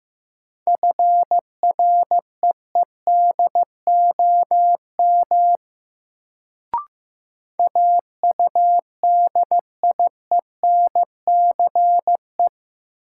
Morse code, heard as a keyed tone, about 15 wpm, 700 Hz. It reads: FREEDOM AUDIENCE
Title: Sentences from Top 800 Words - New Word Drill - 15wpm